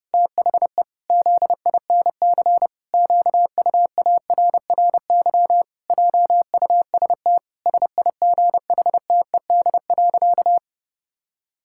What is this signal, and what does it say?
Morse code, heard as a keyed tone, about 30 wpm, 700 Hz. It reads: THE ZINC QUARRY JUST SIGHTED.
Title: Pangram Challenge v2 - 30wpm